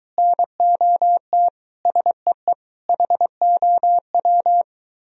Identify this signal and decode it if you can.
Morse code, heard as a keyed tone, about 23 wpm, 700 Hz. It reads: NOT SEE HOW